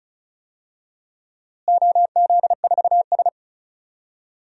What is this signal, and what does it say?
Morse code, heard as a keyed tone, about 36 wpm, 700 Hz. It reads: OZ4S
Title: DXCC-100 Call Signs - 4 Characters - Repeated 3x - 35wpm